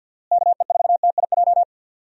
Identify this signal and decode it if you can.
Morse code, heard as a keyed tone, about 47 wpm, 700 Hz. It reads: KE4TIJ
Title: USA Call Signs - 6 Characters - Repeated 2x - 50wpm